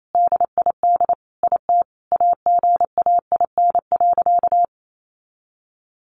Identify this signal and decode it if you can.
Morse code, heard as a keyed tone, about 28 wpm, 700 Hz. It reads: DID IT AGAIN.
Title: Sentences from Top 300 Words 28wpm